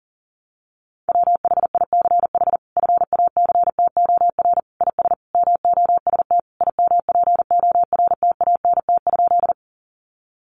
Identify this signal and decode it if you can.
Morse code, heard as a keyed tone, about 40 wpm, 700 Hz. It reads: WHICH FACTOR IS MOST IMPORTANT?
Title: Sentences from Top 1300 Words 40wpm